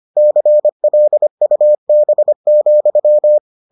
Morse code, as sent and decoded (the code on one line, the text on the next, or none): -.-. .-.. ..- -... --..--
CLUB,